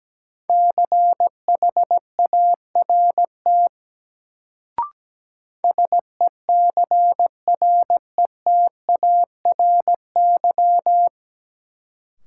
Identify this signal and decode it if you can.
Morse code, heard as a keyed tone, about 17 wpm, 700 Hz. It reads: CHART SECRETARY
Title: Sentences from Top 1300 Words - New Word Drill Alt - 17wpm